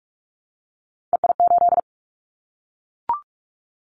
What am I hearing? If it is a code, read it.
Morse code, heard as a keyed tone, about 44 wpm, 700 Hz. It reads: EI8
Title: Callsign Prefixes for Top 50 DXCC - 45wpm